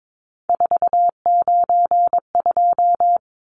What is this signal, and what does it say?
Morse code, heard as a keyed tone, about 22 wpm, 700 Hz. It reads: 492
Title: Numbers - 3-Digit 22wpm